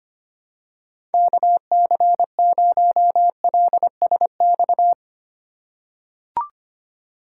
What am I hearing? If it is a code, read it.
Morse code, heard as a keyed tone, about 25 wpm, 700 Hz. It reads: KC0LSX
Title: USA Call Signs - 2x Character Spacing - 25wpm